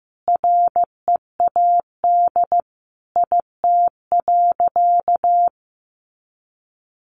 Morse code, heard as a keyed tone, about 15 wpm, 700 Hz. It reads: READ IT.
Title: Sentences from Top 500 Words 15wpm